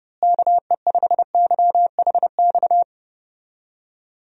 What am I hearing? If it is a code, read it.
Morse code, heard as a keyed tone, about 30 wpm, 700 Hz. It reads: KE5YHX
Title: USA Call Signs - 6 Characters - Repeated 3x - 30wpm